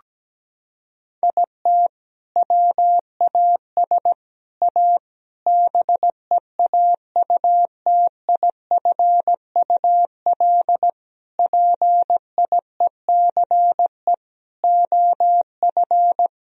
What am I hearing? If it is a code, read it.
Morse code, heard as a keyed tone, about 17 wpm, 700 Hz. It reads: IT WAS A BEAUTIFUL PIECE OF
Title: Sentences from Top 1000 Words - Review Part 2 - 17wpm